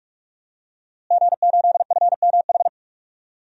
Morse code, heard as a keyed tone, about 44 wpm, 700 Hz. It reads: G8FMH
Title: DXCC-100 Call Signs - 5 Characters - Repeated 3x - 45wpm